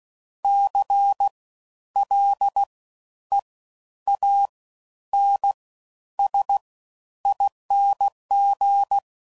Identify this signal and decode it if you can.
Morse code, heard as a keyed tone, about 16 wpm, spaced as for 10 wpm, 800 Hz. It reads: CLEANSING
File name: Suffixes / Special INGs / Fast Ings